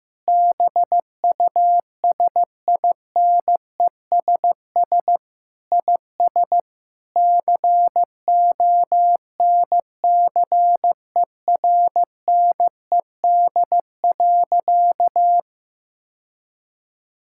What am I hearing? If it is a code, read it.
Morse code, heard as a keyed tone, about 15 wpm, 700 Hz. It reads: BUSINESS IS CONCERNED.
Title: Sentences from Top 1000 Words - Review Part 6 - 15wpm